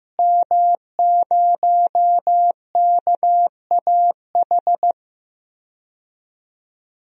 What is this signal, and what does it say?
Morse code, heard as a keyed tone, about 16 wpm, 700 Hz. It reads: M0KAH
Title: DXCC-100 Call Signs - 5 Characters - Repeated 3x - 15wpm